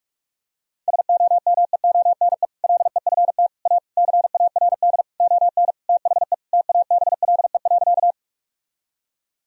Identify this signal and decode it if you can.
Morse code, heard as a keyed tone, about 46 wpm, 700 Hz. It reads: SOMEONE LEFT A CARD ON THE TABLE.
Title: Sentences from Top 1000 Words - Review Part 4 - 45wpm